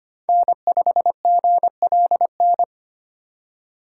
Morse code, heard as a keyed tone, about 25 wpm, 700 Hz. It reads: N5GLN
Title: USA Call Signs - 5 Characters - 3x Character Spacing - 25wpm